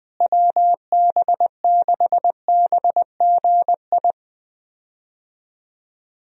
Morse code, heard as a keyed tone, about 21 wpm, 700 Hz. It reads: WB6BGI
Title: USA Call Signs - 6 Characters - Repeated 2x - 20wpm